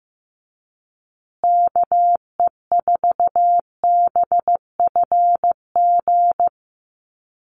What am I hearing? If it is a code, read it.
Morse code, heard as a keyed tone, about 15 wpm, 700 Hz. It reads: KE4BFG
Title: USA Call Signs Repeated 3x 15wpm